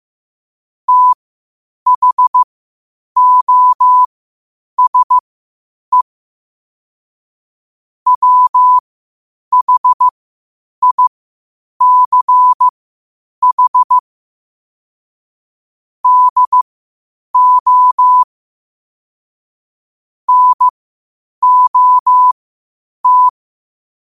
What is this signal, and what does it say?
Morse code, heard as a keyed tone, about 15 wpm, spaced as for 8 wpm, 1000 Hz. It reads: THOSE WHICH DO NOT
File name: qotdfiles / 8wpm